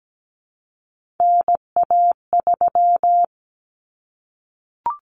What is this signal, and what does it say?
Morse code, heard as a keyed tone, about 17 wpm, 700 Hz. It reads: NA3
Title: Common Cut Numbers - 3-Digit 17wpm